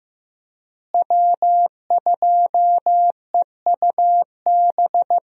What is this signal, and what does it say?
Morse code, heard as a keyed tone, about 15 wpm, 700 Hz. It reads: W2EUB
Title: USA Call Signs - 5 Characters - Repeated 2x - 15wpm